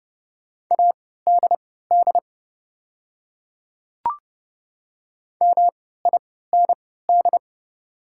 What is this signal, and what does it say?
Morse code, heard as a keyed tone, about 30 wpm, spaced as for 17 wpm, 700 Hz. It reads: ADD MIND
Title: 3x Character Spacing - Top 500 Words - 30wpm